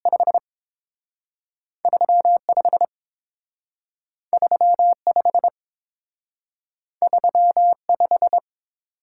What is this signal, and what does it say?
Morse code, heard as a keyed tone, about 30 wpm, spaced as for 6 wpm, 700 Hz. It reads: 53 53 53 5